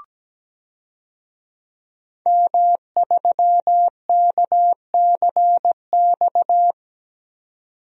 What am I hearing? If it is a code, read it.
Morse code, heard as a keyed tone, about 17 wpm, 700 Hz. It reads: M3KCX
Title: DXCC-100 Call Signs - 5 Characters - 17wpm